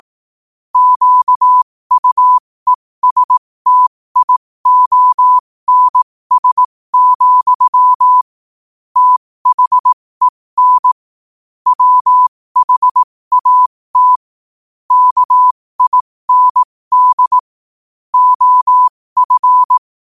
Morse code, heard as a keyed tone, about 18 wpm, spaced as for 15 wpm, 1000 Hz. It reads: QUESTIONS, THEN WHAT KIND OF
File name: qotdfiles / 15wpm